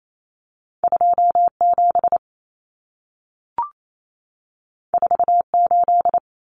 Morse code, heard as a keyed tone, about 28 wpm, 700 Hz. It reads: 27 48
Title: Common Cut Numbers - 2-Digit 28wpm